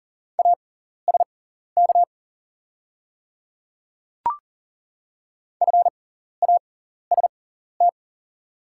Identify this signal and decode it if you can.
Morse code, heard as a keyed tone, about 40 wpm, spaced as for 14 wpm, 690 Hz. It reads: ASK FAST
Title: 6x Character Spacing - Top 500 Words - 40wpm